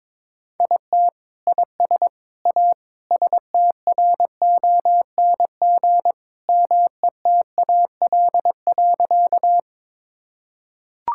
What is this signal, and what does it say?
Morse code, heard as a keyed tone, about 22 wpm, 700 Hz. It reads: IT IS A STRONG METAL.
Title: Sentences from Top 1000 Words - Review Part 6 - 22wpm